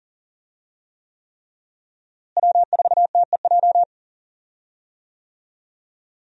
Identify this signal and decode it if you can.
Morse code, heard as a keyed tone, about 40 wpm, 700 Hz. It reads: W4TEJ
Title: USA Call Signs - 5 Characters - Repeated 3x - 40wpm